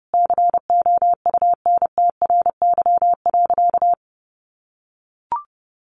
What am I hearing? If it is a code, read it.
Morse code, heard as a keyed tone, about 30 wpm, 700 Hz. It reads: COUNTRY.
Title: Sentences from Top 1400 Words 30wpm